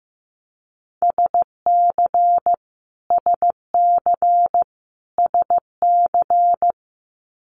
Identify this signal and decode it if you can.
Morse code, heard as a keyed tone, about 15 wpm, 700 Hz. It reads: SC SC SC
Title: Multi-letter Phonemes Repeated 3x 15wpm